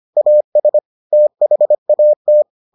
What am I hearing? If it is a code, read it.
Morse code, heard as a keyed tone, about 25 wpm, 600 Hz. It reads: AS THAT